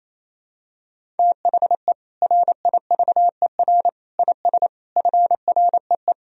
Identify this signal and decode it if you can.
Morse code, heard as a keyed tone, about 28 wpm, 700 Hz. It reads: THE RIVER IS FREE
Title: Sentences from Top 1000 Words - Review Part 1 - 28wpm